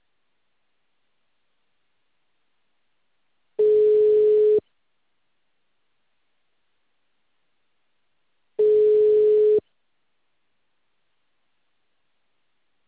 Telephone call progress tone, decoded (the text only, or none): ringback tone